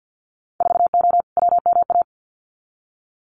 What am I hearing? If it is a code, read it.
Morse code, heard as a keyed tone, about 50 wpm, 700 Hz. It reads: 4O WMA
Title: Winter Field Day Exchanges Encore 50wpm